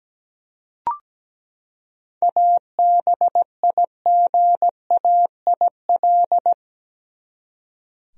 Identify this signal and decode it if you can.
Morse code, heard as a keyed tone, about 17 wpm, 700 Hz. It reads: ABIGAIL
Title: USA Common Names - 17wpm